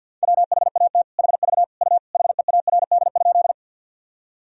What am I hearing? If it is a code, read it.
Morse code, heard as a keyed tone, about 51 wpm, 700 Hz. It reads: WHAT HV U HEARD?
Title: Sentences from Top 400 Words and Abbreviations 50wpm